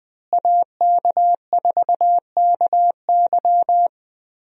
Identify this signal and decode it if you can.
Morse code, heard as a keyed tone, about 20 wpm, 700 Hz. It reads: AK4KY
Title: USA Call Signs - 5 Characters - Repeated 3x - 20wpm